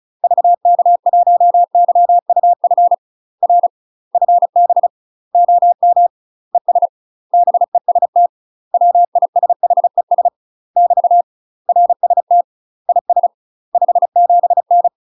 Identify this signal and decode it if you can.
Morse code, heard as a keyed tone, about 36 wpm, 700 Hz. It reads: VK1YUF R FB OM ES BEST WISHES <BT> RST IS 57N
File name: UK MASTER QSOs TRAINING / 35WPM / UK MASTER QSOs 0035